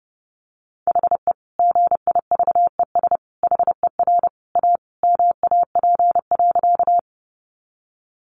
Morse code, heard as a keyed tone, about 30 wpm, 700 Hz. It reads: HE GIVES HER A MAP.
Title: Additional Easy Phrases 30wpm